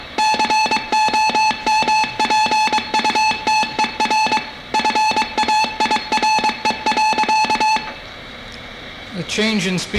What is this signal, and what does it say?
Morse code, heard as a keyed tone, about 23 wpm, 860 Hz, with speech. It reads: COMPUTER FAIRE.